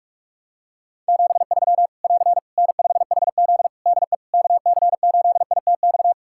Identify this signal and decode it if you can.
Morse code, heard as a keyed tone, about 44 wpm, 700 Hz. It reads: 73 <AR> N5HZ DE KC8ITX
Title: Practice QSO - Exchange 4 - Ending - 45wpm